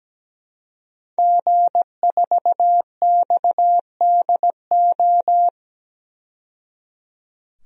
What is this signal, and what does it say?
Morse code, heard as a keyed tone, about 17 wpm, 700 Hz. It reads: G4XDO
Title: DXCC-100 Call Signs - 5 Characters - 17wpm